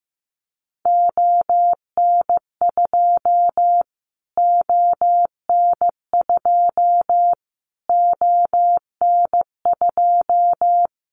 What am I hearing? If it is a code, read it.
Morse code, heard as a keyed tone, about 15 wpm, 700 Hz. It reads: ON2 ON2 ON2
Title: Callsign Prefixes for Top 25 DXCC - Repeated 3x - 15wpm